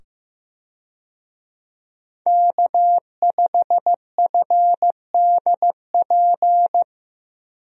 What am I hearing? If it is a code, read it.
Morse code, heard as a keyed tone, about 15 wpm, 700 Hz. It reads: K5FDP
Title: USA Call Signs - 5 Characters - 15wpm